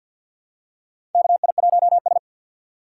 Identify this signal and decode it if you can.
Morse code, heard as a keyed tone, about 51 wpm, 690 Hz. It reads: KI1S
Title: USA Call Signs - 4 Characters - 6x Character Spacing - 50wpm